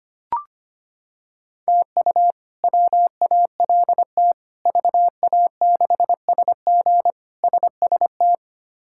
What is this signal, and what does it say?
Morse code, heard as a keyed tone, about 25 wpm, 700 Hz. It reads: TU WALT VA6SG SST
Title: K1USN SST QSO Practice - 25wpm